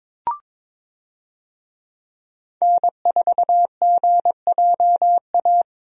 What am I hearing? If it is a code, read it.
Morse code, heard as a keyed tone, about 22 wpm, 700 Hz. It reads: N4GJA
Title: USA Call Signs - 5 Characters - 22wpm